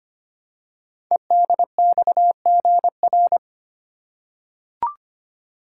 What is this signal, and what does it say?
Morse code, heard as a keyed tone, about 25 wpm, 700 Hz. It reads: EDXGR
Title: Code Groups - 5 Letters - 25wpm